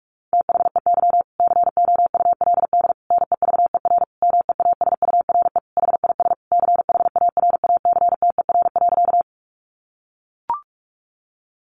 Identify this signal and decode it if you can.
Morse code, heard as a keyed tone, about 44 wpm, 700 Hz. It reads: THEY COULD NEVER MEASURE HIS CHARACTER.
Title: Sentences from Top 1200 Words 45wpm